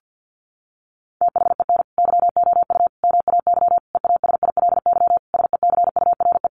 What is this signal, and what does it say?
Morse code, heard as a keyed tone, about 50 wpm, 700 Hz. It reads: THEN YOU MAY EASILY SECURE
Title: Sentences from Top 1200 Words 50wpm